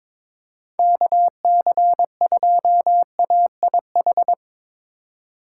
Morse code, heard as a keyed tone, about 21 wpm, 700 Hz. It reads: KC2AIH
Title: USA Call Signs - 6 Characters - 3x Character Spacing - 22wpm